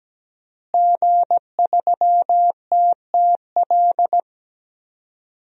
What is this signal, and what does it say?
Morse code, heard as a keyed tone, about 17 wpm, 700 Hz. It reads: G3TTL
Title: DXCC-100 Call Signs - 5 Characters - Repeated 3x - 17wpm